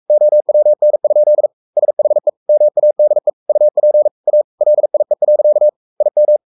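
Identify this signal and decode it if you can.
Morse code, heard as a keyed tone, about 44 wpm, 600 Hz. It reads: OWN? SHE MADE UP A LIE. IM
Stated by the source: Morse code at 43 wpm sending Ken Follett's novel Never in English, from CW med Gnister.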